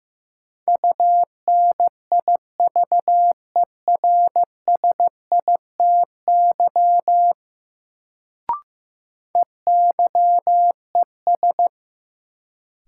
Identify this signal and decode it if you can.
Morse code, heard as a keyed tone, about 16 wpm, 700 Hz. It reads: UNIVERSITY EYES
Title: Sentences from Top 1000 Words - Review Part 4 - New Word Drill - 15wpm